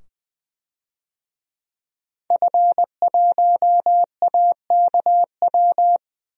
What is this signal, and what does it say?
Morse code, heard as a keyed tone, about 20 wpm, 700 Hz. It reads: F1AKW
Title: DXCC-100 Call Signs - 5 Characters - 20wpm